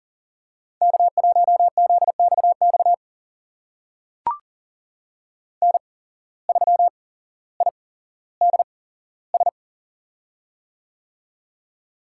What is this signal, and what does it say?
Morse code, heard as a keyed tone, about 40 wpm, spaced as for 17 wpm, 700 Hz. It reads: K1ZXX N3IDS
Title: USA Call Signs - 5 Characters - 8x Character Spacing - 40wpm